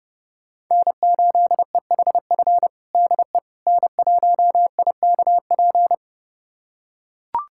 Morse code, heard as a keyed tone, about 30 wpm, 700 Hz. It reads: N8EHF DE N1IKP
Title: USA Callsign DE Callsign - 5 Characters - 30wpm